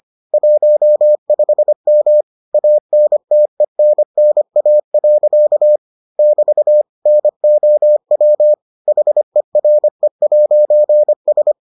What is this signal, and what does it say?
Morse code, heard as a keyed tone, about 25 wpm, 600 Hz. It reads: 15M ANTENNA. <BT> NOW HERE'S